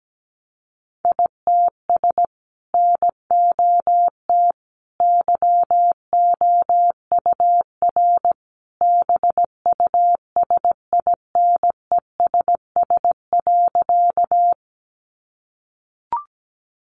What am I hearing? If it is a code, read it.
Morse code, heard as a keyed tone, about 17 wpm, 700 Hz. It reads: ITS NOT YOUR BUSINESS.
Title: Sentences from Top 600 Words 17wpm